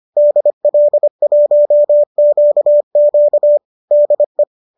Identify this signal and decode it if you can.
Morse code, heard as a keyed tone, about 25 wpm, 590 Hz. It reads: DL1QQ DE
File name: qso219 25